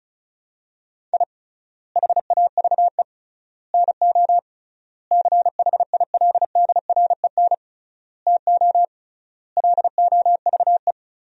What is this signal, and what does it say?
Morse code, heard as a keyed tone, about 34 wpm, 700 Hz. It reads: I HAVE NO CHILDREN TO LOVE
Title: 3x Word-Spacing - Sentences from Top 400 Words - 35wpm